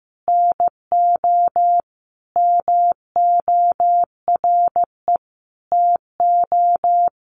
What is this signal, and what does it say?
Morse code, heard as a keyed tone, about 15 wpm, 700 Hz. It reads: NO MORE TO
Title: Sentences from Top 100 Words 15wpm